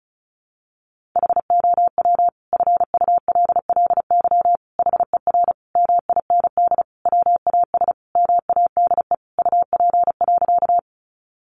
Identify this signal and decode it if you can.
Morse code, heard as a keyed tone, about 34 wpm, 700 Hz. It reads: HOW FULLY HER MIND WAS MADE UP.
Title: Sentences from Top 1300 Words 35wpm